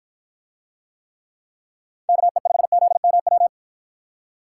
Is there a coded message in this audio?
Yes: Morse code from 2.1 to 3.5 s, heard as a keyed tone, about 49 wpm, 700 Hz. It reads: KE5ZMW